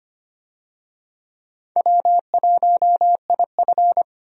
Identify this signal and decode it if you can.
Morse code, heard as a keyed tone, about 25 wpm, 700 Hz. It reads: W1IF
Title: USA Call Signs - 4 Characters - Repeated 3x - 25wpm